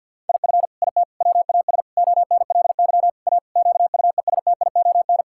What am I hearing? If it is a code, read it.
Morse code, heard as a keyed tone, about 50 wpm, 700 Hz. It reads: IF IT WAS ONLY A QUESTION
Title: Sentences from Top 1600 Words 50wpm